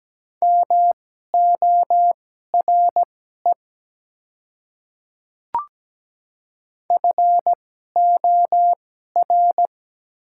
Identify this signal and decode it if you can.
Morse code, heard as a keyed tone, about 17 wpm, spaced as for 12 wpm, 700 Hz. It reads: MORE FOR
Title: 2x Character Spacing - Top 100 Words - 17wpm